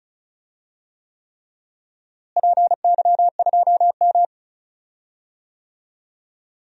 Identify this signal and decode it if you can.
Morse code, heard as a keyed tone, about 36 wpm, 700 Hz. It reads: PY2M
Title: DXCC-100 Call Signs - 4 Characters - Repeated 3x - 35wpm